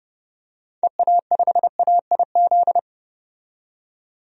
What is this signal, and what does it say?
Morse code, heard as a keyed tone, about 30 wpm, 690 Hz. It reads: EA5AIZ
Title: DXCC-100 Call Signs - 6 Characters - 30wpm